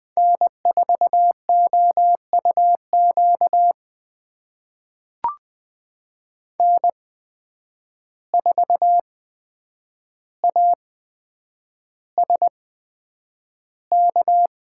Morse code, heard as a keyed tone, about 20 wpm, spaced as for 9 wpm, 700 Hz. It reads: N4OUQ N4ASK
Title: USA Call Signs - 5 Characters - 8x Character Spacing - 20wpm